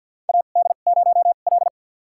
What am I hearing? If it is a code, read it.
Morse code, heard as a keyed tone, about 51 wpm, spaced as for 36 wpm, 700 Hz. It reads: AD0L
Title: USA Call Signs - 4 Characters - 2x Character Spacing - 50wpm